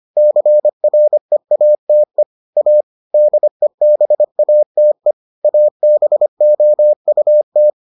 Morse code, heard as a keyed tone, about 25 wpm, 600 Hz. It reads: CREATE A DEBATE ABOUT